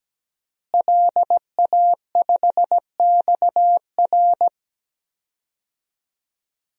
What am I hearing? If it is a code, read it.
Morse code, heard as a keyed tone, about 17 wpm, 700 Hz. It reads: LA5XR
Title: DXCC-100 Call Signs - 5 Characters - Repeated 3x - 17wpm